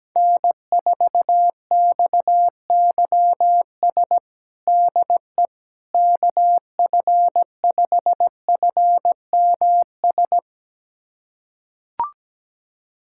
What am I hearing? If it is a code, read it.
Morse code, heard as a keyed tone, about 17 wpm, 700 Hz. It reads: N4XYS DE KF5FMS
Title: USA Callsign DE Callsign 17wpm